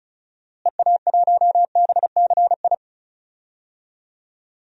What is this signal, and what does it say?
Morse code, heard as a keyed tone, about 34 wpm, 700 Hz. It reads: EA1BCI
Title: DXCC-100 Call Signs - 6 Characters - Repeated 3x - 35wpm